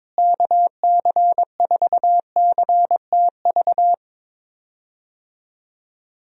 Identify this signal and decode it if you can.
Morse code, heard as a keyed tone, about 22 wpm, 700 Hz. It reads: KC4CTV
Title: USA Call Signs - 6 Characters - Repeated 2x - 22wpm